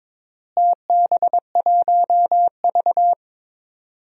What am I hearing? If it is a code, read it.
Morse code, heard as a keyed tone, about 22 wpm, 700 Hz. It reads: TB1V
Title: DXCC-100 Call Signs - 4 Characters - Repeated 2x - 22wpm